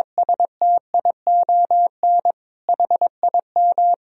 Morse code, heard as a keyed tone, about 22 wpm, 700 Hz. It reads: STION HIM
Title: Sentences from Top 1000 Words - Review Part 2 - 22wpm